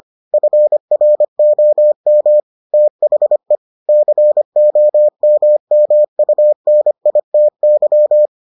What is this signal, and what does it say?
Morse code, heard as a keyed tone, about 25 wpm, 600 Hz. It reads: FROM THE COMMUNITY